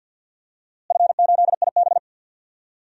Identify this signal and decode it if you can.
Morse code, heard as a keyed tone, about 51 wpm, 700 Hz. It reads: F8IB
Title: DXCC-100 Call Signs - 4 Characters - Repeated 3x - 50wpm